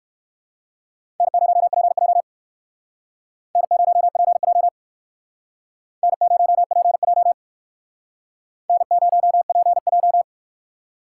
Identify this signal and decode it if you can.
Morse code, heard as a keyed tone, about 56 wpm, 700 Hz. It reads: N0PJ N0PJ N0PJ N0PJ